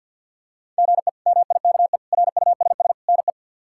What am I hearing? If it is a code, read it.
Morse code, heard as a keyed tone, about 49 wpm, 700 Hz. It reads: GE MIKE RUSS NE
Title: K1USN SST QSO Practice - 50wpm